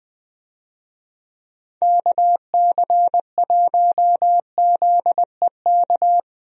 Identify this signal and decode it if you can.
Morse code, heard as a keyed tone, about 20 wpm, 700 Hz. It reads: KC1ZEK
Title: USA Call Signs - 6 Characters - Repeated 3x - 20wpm